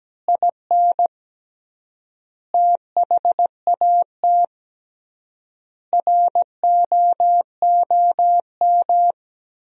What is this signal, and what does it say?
Morse code, heard as a keyed tone, about 17 wpm, 700 Hz. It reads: IN THAT ROOM
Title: 3x Word-Spacing - Sentences from Top 500 Words - 17wpm